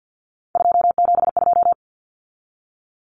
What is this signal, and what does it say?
Morse code, heard as a keyed tone, about 50 wpm, 700 Hz. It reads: 272
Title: Numbers - 3-Digit 50wpm